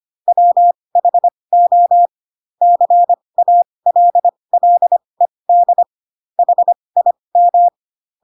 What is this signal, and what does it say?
Morse code, heard as a keyed tone, about 25 wpm, spaced as for 20 wpm, 700 Hz. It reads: WHO CALLED HIM
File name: Pinocchio / Pinocchio ch2 15 25 ik1hge